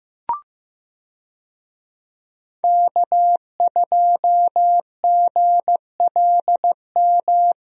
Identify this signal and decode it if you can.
Morse code, heard as a keyed tone, about 15 wpm, 700 Hz. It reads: K2GLM
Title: USA Call Signs - 5 Characters - Repeated 3x - 15wpm